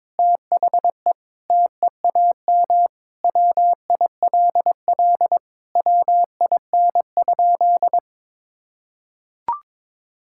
Morse code, heard as a keyed tone, about 21 wpm, 700 Hz. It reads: THE TEAM WILL WIN?
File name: practice / 22wpm / Sentences with 4-Word Phrases - 22wpm